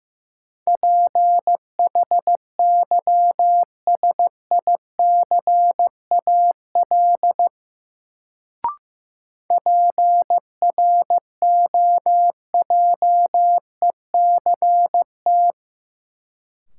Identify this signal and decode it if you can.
Morse code, heard as a keyed tone, about 15 wpm, 700 Hz. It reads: PHYSICAL PROJECT
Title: Sentences from Top 1400 Words - New Word Drill Alt - 15wpm